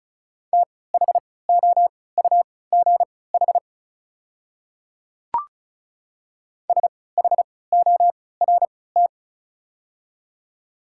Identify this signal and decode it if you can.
Morse code, heard as a keyed tone, about 34 wpm, spaced as for 20 wpm, 690 Hz. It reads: THOUGH SHORT
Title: 3x Character Spacing - Top 500 Words - 35wpm